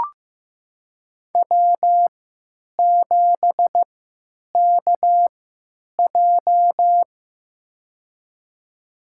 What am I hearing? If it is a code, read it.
Morse code, heard as a keyed tone, about 15 wpm, spaced as for 9 wpm, 700 Hz. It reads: W7KJ
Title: USA Call Signs - 3x Character Spacing - 15wpm